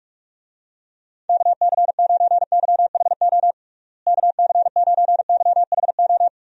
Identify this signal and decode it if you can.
Morse code, heard as a keyed tone, about 46 wpm, 700 Hz. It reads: KC9YHO KC9YHO
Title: USA Call Signs - 6 Characters - Repeated 2x - 45wpm